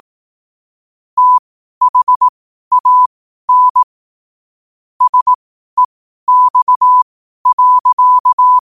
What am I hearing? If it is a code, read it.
Morse code, heard as a keyed tone, about 18 wpm, spaced as for 12 wpm, 1000 Hz. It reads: THAN SEX.
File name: qotdfiles / 12wpm